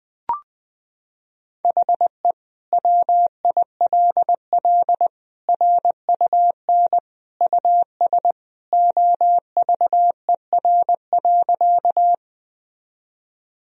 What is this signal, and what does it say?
Morse code, heard as a keyed tone, about 21 wpm, 700 Hz. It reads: HE WILL RUN US OVER.
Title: Sentences from Top 1000 Words - Review Part 1 - 20wpm